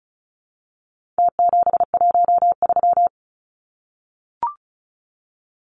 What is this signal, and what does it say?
Morse code, heard as a keyed tone, about 34 wpm, 700 Hz. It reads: T713
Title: Numbers - Contest Style - 4-Digit 35wpm